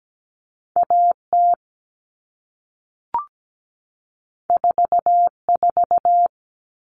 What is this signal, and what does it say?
Morse code, heard as a keyed tone, about 17 wpm, 700 Hz. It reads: AT 44
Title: All Cut Numbers - 2-Digit 17wpm